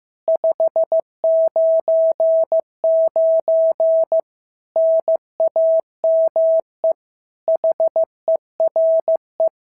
Morse code, heard as a keyed tone, about 14 wpm, 650 Hz. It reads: 599 NAME HERE
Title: SOTA QSO Practice 15wpm